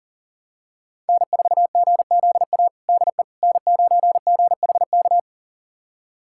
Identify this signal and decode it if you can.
Morse code, heard as a keyed tone, about 40 wpm, 700 Hz. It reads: N4GZA DE N9GHK
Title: USA Callsign DE Callsign - 5 Characters - 40wpm